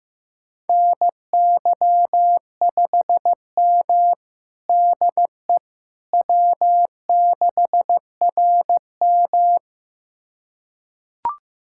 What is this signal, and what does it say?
Morse code, heard as a keyed tone, about 15 wpm, 700 Hz. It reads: NY5M DE W6RM
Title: USA Callsign DE Callsign - 4 Characters - 15wpm